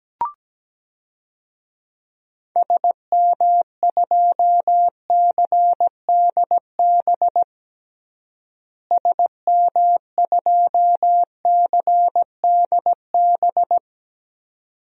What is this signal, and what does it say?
Morse code, heard as a keyed tone, about 17 wpm, 700 Hz. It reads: SM2CDB SM2CDB
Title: DXCC-100 Call Signs - 6 Characters - Repeated 2x - 17wpm